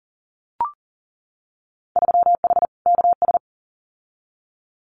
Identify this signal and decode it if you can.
Morse code, heard as a keyed tone, about 40 wpm, 700 Hz. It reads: 3H KS
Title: Winter Field Day Exchanges 40wpm